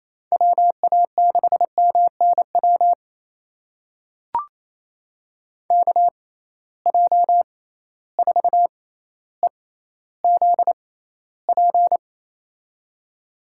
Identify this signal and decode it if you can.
Morse code, heard as a keyed tone, about 28 wpm, spaced as for 14 wpm, 700 Hz. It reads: WA6MNW KJ4EZP
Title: USA Call Signs - 6 Characters - 6x Character Spacing - 28wpm